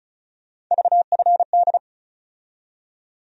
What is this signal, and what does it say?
Morse code, heard as a keyed tone, about 36 wpm, 700 Hz. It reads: VFD
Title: USA Call Sign Postfixes - 3 Letters 35wpm